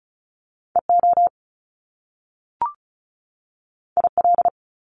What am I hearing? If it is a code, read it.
Morse code, heard as a keyed tone, about 36 wpm, 700 Hz. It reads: EO IL
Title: Top Two Letter Combinations in Words 35wpm